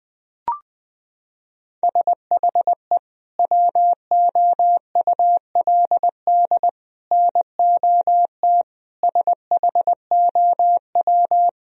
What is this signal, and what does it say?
Morse code, heard as a keyed tone, about 20 wpm, 700 Hz. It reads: SHE WOULD NOT SHOW
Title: Sentences from Top 1000 Words - Review Part 3 - 20wpm